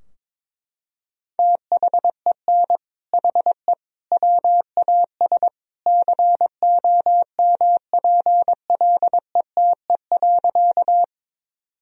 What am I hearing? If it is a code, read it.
Morse code, heard as a keyed tone, about 22 wpm, 700 Hz. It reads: THEN HE WAS COMPLETE.
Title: Sentences from Top 1000 Words - Review Part 2 - 22wpm